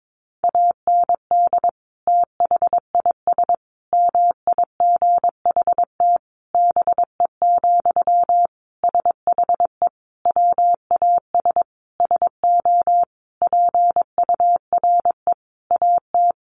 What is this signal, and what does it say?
Morse code, heard as a keyed tone, about 22 wpm, 700 Hz. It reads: AND THIS MIGHT BE, SHE WAS SO PURE AT
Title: Sentences from Top 1500 Words 22wpm